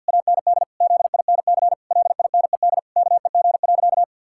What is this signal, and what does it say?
Morse code, heard as a keyed tone, about 50 wpm, 700 Hz. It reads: AND ZINC LINED KEG.
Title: Pangram Challenge v2 - 50wpm